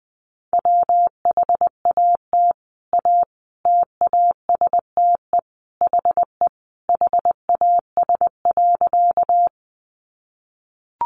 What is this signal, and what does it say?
Morse code, heard as a keyed tone, about 20 wpm, 700 Hz. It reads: WHAT A TASTE HE HAS.
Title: Sentences from Top 1100 Words 20wpm